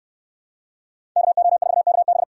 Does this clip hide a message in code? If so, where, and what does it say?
Yes: Morse code from 1.2 to 2.2 s, heard as a keyed tone, about 68 wpm, 700 Hz. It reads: KO4KB